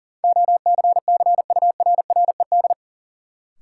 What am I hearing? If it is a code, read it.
Morse code, heard as a keyed tone, about 39 wpm, 700 Hz. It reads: OCCURRED
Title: Sets of 5 Words Encore - New Word Drill - 40wpm